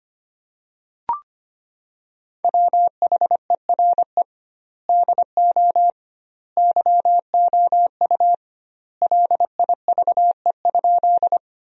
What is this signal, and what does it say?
Morse code, heard as a keyed tone, about 25 wpm, 700 Hz. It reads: WHERE DO YOU LIVE?
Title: 2x Word-Spacing - Sentences from Top 200 Words - 25wpm